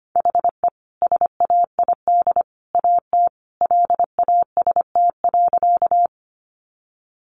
Morse code, heard as a keyed tone, about 25 wpm, 700 Hz. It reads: HE SAID AT LAST.
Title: Sentences from Top 300 Words 25wpm